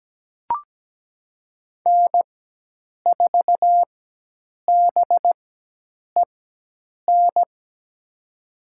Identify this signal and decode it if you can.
Morse code, heard as a keyed tone, about 17 wpm, spaced as for 8 wpm, 700 Hz. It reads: N4BEN
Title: USA Call Signs - 5 Characters - 4x Character Spacing - 17wpm